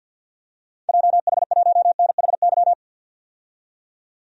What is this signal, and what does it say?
Morse code, heard as a keyed tone, about 50 wpm, 700 Hz. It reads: JH1NHY
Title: DXCC-100 Call Signs - 6 Characters - 50wpm